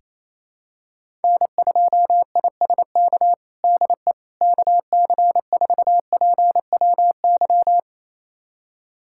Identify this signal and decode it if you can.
Morse code, heard as a keyed tone, about 28 wpm, 700 Hz. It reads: N2ISK DE KC4PWY
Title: USA Callsign DE Callsign 28wpm